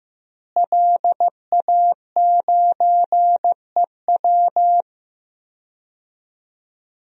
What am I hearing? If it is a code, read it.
Morse code, heard as a keyed tone, about 15 wpm, 700 Hz. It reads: LA9EW
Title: DXCC-100 Call Signs - 5 Characters - Repeated 2x - 15wpm